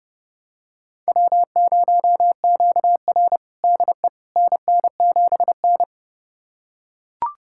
Morse code, heard as a keyed tone, about 32 wpm, 700 Hz. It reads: W0QR DE NN7N
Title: USA Callsign DE Callsign - 4 Characters - 30wpm